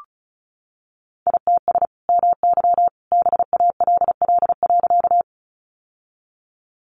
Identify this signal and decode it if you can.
Morse code, heard as a keyed tone, about 34 wpm, 700 Hz. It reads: ITS MY BALL.
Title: Sentences from Top 900 Words 35wpm